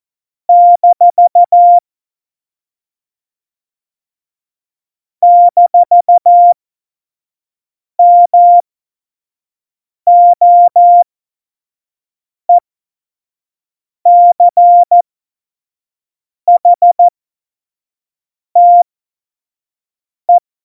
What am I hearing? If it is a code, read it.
Morse code, heard as a keyed tone, about 14 wpm, spaced as for 5 wpm, 700 Hz. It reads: - - MOECHTE
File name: Morsekurs / Witze / Witz 03.Dezember 2019, 25 F70